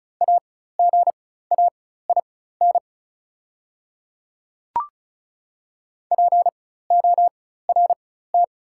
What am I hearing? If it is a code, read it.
Morse code, heard as a keyed tone, about 36 wpm, spaced as for 16 wpm, 700 Hz. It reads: AGAIN PORT
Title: 4x Character Spacing - Top 300 Words - 35wpm